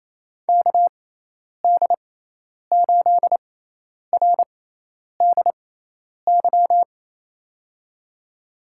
Morse code, heard as a keyed tone, about 27 wpm, spaced as for 10 wpm, 700 Hz. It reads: KD8RDY
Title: USA Call Signs - 6 Characters - 6x Character Spacing - 28wpm